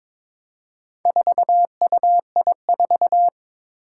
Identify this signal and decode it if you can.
Morse code, heard as a keyed tone, about 22 wpm, 700 Hz. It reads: 4UI4
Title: Four Challenging Letter-Number Combinations 22wpm